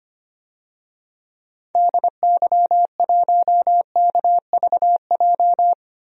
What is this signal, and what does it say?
Morse code, heard as a keyed tone, about 25 wpm, 700 Hz. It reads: DY1KVJ
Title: DXCC-100 Call Signs - 6 Characters - 25wpm